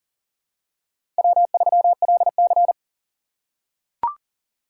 Morse code, heard as a keyed tone, about 40 wpm, 700 Hz. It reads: W3LC
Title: USA Call Signs - 4 Characters - 2x Character Spacing - 40wpm